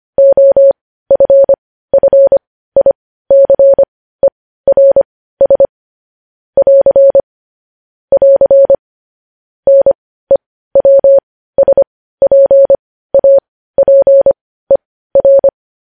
Morse code, heard as a keyed tone, about 25 wpm, spaced as for 15 wpm, 550 Hz. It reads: OFFICERS <AR> <AR> NEWSPAPER
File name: podcast / News headlines-1x-25x15